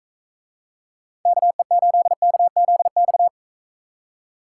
Morse code, heard as a keyed tone, about 45 wpm, 700 Hz. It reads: KE8KZX